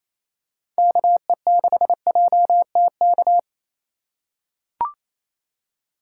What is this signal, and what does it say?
Morse code, heard as a keyed tone, about 28 wpm, 700 Hz. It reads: KE6JTK
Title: USA Call Signs - 4x Character Spacing - 28wpm